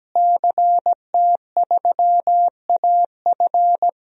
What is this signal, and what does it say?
Morse code, heard as a keyed tone, about 17 wpm, 700 Hz. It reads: CT3AF
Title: DXCC-100 Call Signs - 5 Characters - Repeated 2x - 17wpm